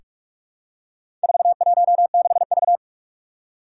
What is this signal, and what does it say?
Morse code, heard as a keyed tone, about 44 wpm, 700 Hz. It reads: 416V
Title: Four Challenging Letter-Number Combinations 45wpm